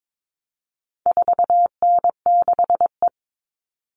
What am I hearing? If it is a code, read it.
Morse code, heard as a keyed tone, about 22 wpm, 700 Hz. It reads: 4N6E
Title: All Cut Numbers - 4-Digit 22wpm